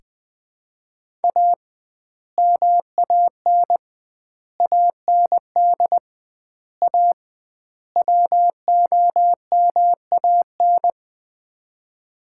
Morse code, heard as a keyed tone, about 20 wpm, 700 Hz. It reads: A MAN AND A WOMAN
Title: 2x Word-Spacing - Sets of 5 Words - 20wpm